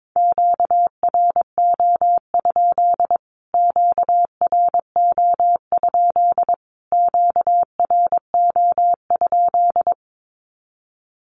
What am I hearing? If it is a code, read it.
Morse code, heard as a keyed tone, about 22 wpm, 700 Hz. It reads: QRO? QRO? QRO?
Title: Q-Codes Repeated 3x 22wpm